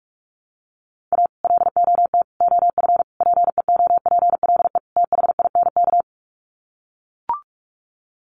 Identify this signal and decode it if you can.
Morse code, heard as a keyed tone, about 44 wpm, 700 Hz. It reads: A LOT OF PEOPLE THINK
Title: Sets of 5 Words Encore 45wpm